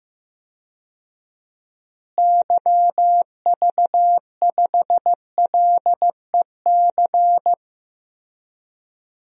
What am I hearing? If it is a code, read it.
Morse code, heard as a keyed tone, about 16 wpm, 700 Hz. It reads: YV5LEC
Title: DXCC-100 Call Signs - 6 Characters - Repeated 3x - 15wpm